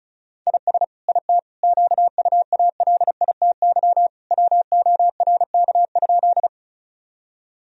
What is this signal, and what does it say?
Morse code, heard as a keyed tone, about 34 wpm, 700 Hz. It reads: IS IT QUALITY WORK?
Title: Sentences from Top 1000 Words - Review Part 4 - 35wpm